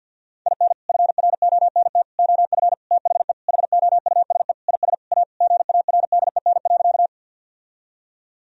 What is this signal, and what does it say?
Morse code, heard as a keyed tone, about 50 wpm, 700 Hz. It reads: IN FRONT OF THE HOUSE IS A GARDEN.
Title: Sentences from Top 1000 Words - Review Part 4 - 50wpm